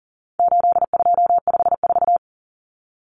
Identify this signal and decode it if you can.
Morse code, heard as a keyed tone, about 40 wpm, 700 Hz. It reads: 8254
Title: Common Cut Numbers - 4-Digit 40wpm